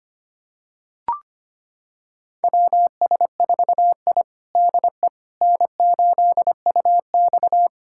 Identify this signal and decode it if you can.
Morse code, heard as a keyed tone, about 25 wpm, 700 Hz. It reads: WS4I DE N8UX
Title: USA Callsign DE Callsign - 4 Characters - 25wpm